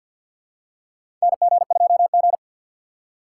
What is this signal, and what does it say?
Morse code, heard as a keyed tone, about 47 wpm, 700 Hz. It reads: NG2G
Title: USA Call Signs - 4 Characters - 3x Character Spacing - 50wpm